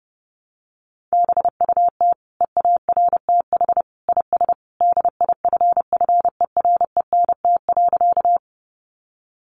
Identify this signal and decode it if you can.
Morse code, heard as a keyed tone, about 30 wpm, 700 Hz. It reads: BUT EARTH IS DIFFERENT.
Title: Sentences from Top 600 Words 30wpm